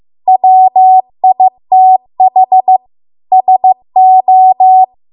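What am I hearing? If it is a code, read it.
Morse code, heard as a keyed tone, about 15 wpm, 750 Hz. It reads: WITH SO
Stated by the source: Morse code practice at 15 wpm from the national amateur radio association's code archive.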